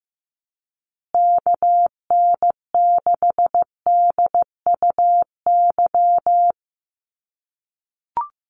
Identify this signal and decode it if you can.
Morse code, heard as a keyed tone, about 15 wpm, 700 Hz. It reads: KN6DUY
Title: USA Call Signs Repeated 3x 15wpm